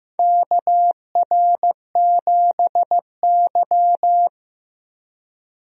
Morse code, heard as a keyed tone, about 15 wpm, 700 Hz. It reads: KR7Y
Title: USA Call Signs - 4 Characters - Repeated 3x - 15wpm